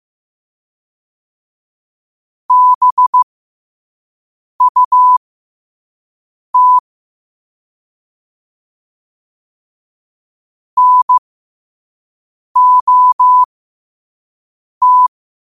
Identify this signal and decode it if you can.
Morse code, heard as a keyed tone, about 15 wpm, spaced as for 5 wpm, 1000 Hz. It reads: BUT NOT
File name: qotdfiles / 5wpm